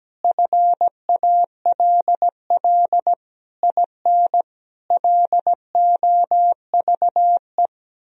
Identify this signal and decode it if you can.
Morse code, heard as a keyed tone, about 17 wpm, 700 Hz. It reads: FALL IN LOVE
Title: Sentences from Top 1000 Words - Review Part 4 - 17wpm